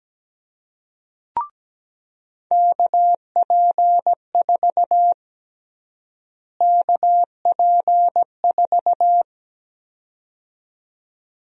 Morse code, heard as a keyed tone, about 17 wpm, 700 Hz. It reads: KP4 KP4
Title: USA Call Sign Prefixes - Repeated 2x - 17wpm